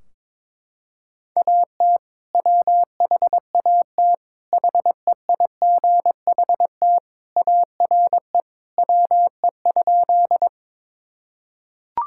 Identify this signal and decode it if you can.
Morse code, heard as a keyed tone, about 22 wpm, 700 Hz. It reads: AT WHAT HEIGHT ARE WE?
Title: Sentences from Top 1600 Words 22wpm